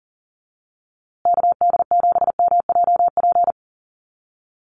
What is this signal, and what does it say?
Morse code, heard as a keyed tone, about 40 wpm, 700 Hz. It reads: KD7MJP
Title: USA Call Signs Repeated 3x 40wpm